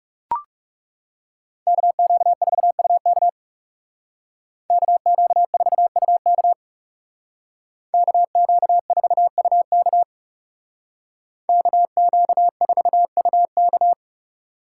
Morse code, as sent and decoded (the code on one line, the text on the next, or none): -.- --.- ....- ..- -.- / -.- --.- ....- ..- -.- / -.- --.- ....- ..- -.- / -.- --.- ....- ..- -.-
KQ4UK KQ4UK KQ4UK KQ4UK